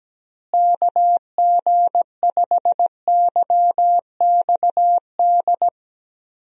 Morse code, heard as a keyed tone, about 17 wpm, 700 Hz. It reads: KG5YXD
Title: USA Call Signs - 6 Characters - Repeated 3x - 17wpm